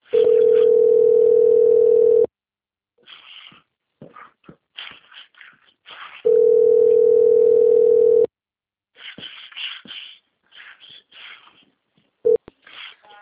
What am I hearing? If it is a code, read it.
Ringback tone (North America), 440 + 480 Hz.